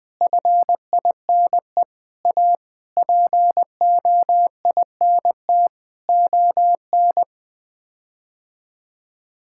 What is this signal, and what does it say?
Morse code, heard as a keyed tone, about 20 wpm, 700 Hz. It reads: FINE A POINT ON
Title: Sets of 8 Words - 20wpm